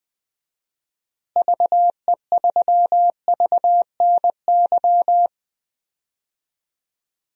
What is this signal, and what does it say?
Morse code, heard as a keyed tone, about 20 wpm, 700 Hz. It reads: VE3VNY
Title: DXCC-100 Call Signs - 6 Characters - Repeated 3x - 20wpm